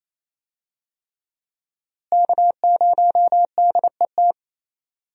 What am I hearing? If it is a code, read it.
Morse code, heard as a keyed tone, about 27 wpm, 700 Hz. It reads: K0DET
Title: USA Call Signs - 5 Characters - Repeated 3x - 28wpm